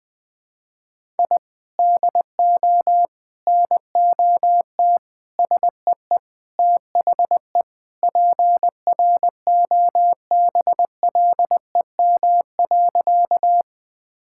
Morse code, heard as a keyed tone, about 20 wpm, 700 Hz. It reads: I DO NOT SEE THE PROBLEM.
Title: Sentences with 4-Word Phrases - 20wpm